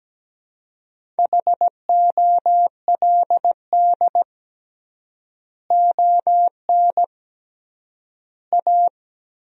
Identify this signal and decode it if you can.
Morse code, heard as a keyed tone, about 17 wpm, 700 Hz. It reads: HOLD ON A
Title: 3x Word-Spacing - Sentences from Top 500 Words - 17wpm